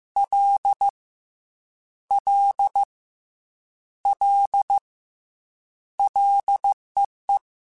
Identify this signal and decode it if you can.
Morse code, heard as a keyed tone, about 15 wpm, 800 Hz. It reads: L L L LEE
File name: Combinations / Combinations-mp3s / Combinations@13